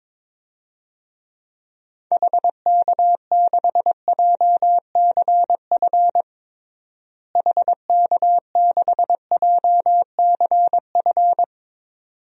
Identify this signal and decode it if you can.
Morse code, heard as a keyed tone, about 21 wpm, 700 Hz. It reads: HK6JCF HK6JCF
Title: DXCC-100 Call Signs - 6 Characters - Repeated 2x - 22wpm